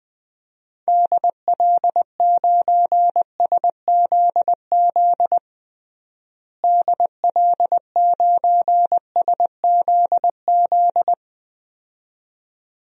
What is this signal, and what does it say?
Morse code, heard as a keyed tone, about 21 wpm, 700 Hz. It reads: DL9SZZ DL9SZZ
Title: DXCC-100 Call Signs - 6 Characters - Repeated 2x - 20wpm